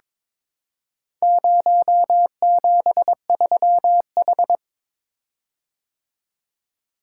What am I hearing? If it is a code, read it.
Morse code, heard as a keyed tone, about 22 wpm, 700 Hz. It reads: 073H
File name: practice / 22wpm / Four Letter-Number Combinations - 22wpm